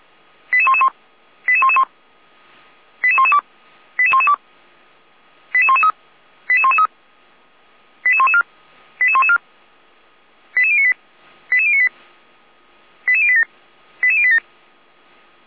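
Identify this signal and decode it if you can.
Five-tone selective call (ZVEI), heard as quick runs of five tones, tones of about 70 ms. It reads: 88111 88111 88112 88112 88113 88113 88114 88114 88098 88098 88087 88087
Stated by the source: ZVEI five-tone selective call received with an SDR.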